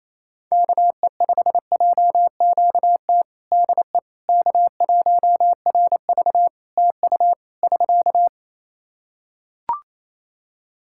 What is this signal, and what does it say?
Morse code, heard as a keyed tone, about 27 wpm, 700 Hz. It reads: KE5JQT DE K1RV TU <SK>
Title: Practice QSO - Exchange 4 - Ending - 28wpm